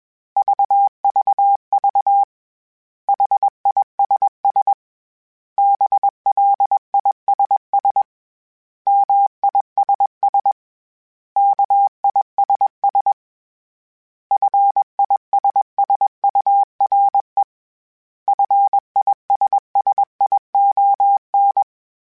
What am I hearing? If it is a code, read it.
Morse code, heard as a keyed tone, about 21 wpm, 800 Hz. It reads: VVV HISS BLISS MISS KISS FISSURE FISSION
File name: General / General-mp3s / Suffixes @20